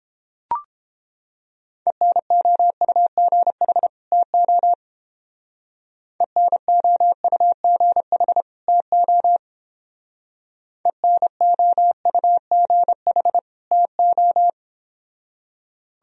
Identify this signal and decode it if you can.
Morse code, heard as a keyed tone, about 29 wpm, 700 Hz. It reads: ENOUGH TO ENOUGH TO ENOUGH TO